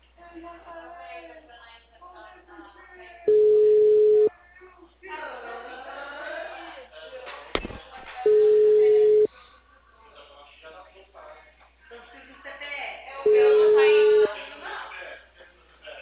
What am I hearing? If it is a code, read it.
Ringback tone (Europe, 425 Hz), with speech.
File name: out- -2001- - - .61443